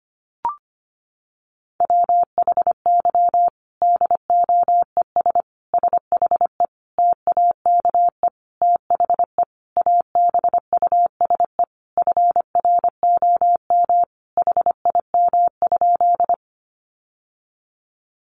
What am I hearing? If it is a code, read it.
Morse code, heard as a keyed tone, about 25 wpm, 700 Hz. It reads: WHY DOES SHE TAKE THE ABUSE FROM HIM?
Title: Sentences from Top 1300 Words 25wpm